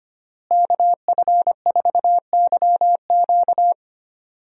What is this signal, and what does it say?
Morse code, heard as a keyed tone, about 25 wpm, 700 Hz. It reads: KF4YQ
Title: USA Call Signs - 5 Characters - Repeated 2x - 25wpm